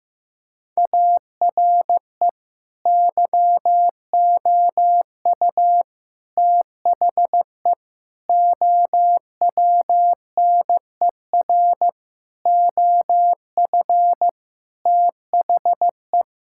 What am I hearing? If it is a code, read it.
Morse code, heard as a keyed tone, about 15 wpm, 700 Hz. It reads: ARE YOU THE OWNER OF THE
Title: Sentences from Top 1600 Words 15wpm